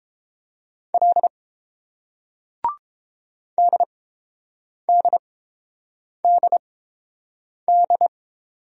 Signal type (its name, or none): Morse code